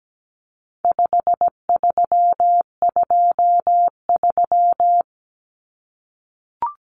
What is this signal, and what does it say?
Morse code, heard as a keyed tone, about 17 wpm, 700 Hz. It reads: 5323
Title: Numbers - 4-Digit 17wpm